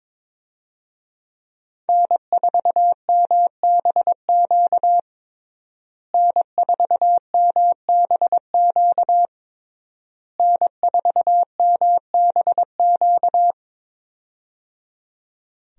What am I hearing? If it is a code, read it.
Morse code, heard as a keyed tone, about 22 wpm, 700 Hz. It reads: N4MBQ N4MBQ N4MBQ
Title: USA Call Signs - 5 Characters - Repeated 3x - 22wpm